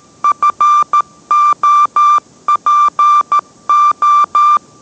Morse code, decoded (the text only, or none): FOPO